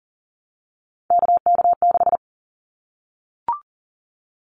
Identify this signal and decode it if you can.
Morse code, heard as a keyed tone, about 40 wpm, 700 Hz. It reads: KK6
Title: Callsign Prefixes for Top 25 DXCC - Repeated 3x - 40wpm